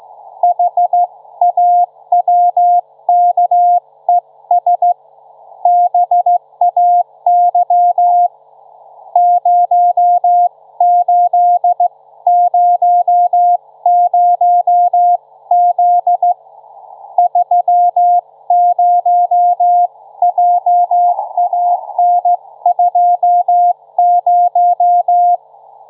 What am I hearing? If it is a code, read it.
Morse code, heard as a keyed tone, about 16 wpm, 710 Hz. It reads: HAWKES BAY 0800Z 30JAN20
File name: Trivia-ZL2WT